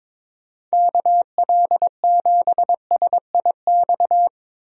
Morse code, heard as a keyed tone, about 22 wpm, 700 Hz. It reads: KL7SIX
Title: DXCC-100 Call Signs - 6 Characters - 22wpm